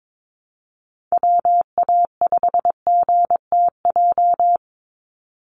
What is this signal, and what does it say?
Morse code, heard as a keyed tone, about 22 wpm, 700 Hz. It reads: WA5GTJ
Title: USA Call Signs 22wpm